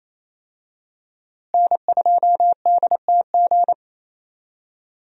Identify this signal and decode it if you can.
Morse code, heard as a keyed tone, about 27 wpm, 700 Hz. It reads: N2DTG
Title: USA Call Signs - 5 Characters - Repeated 3x - 28wpm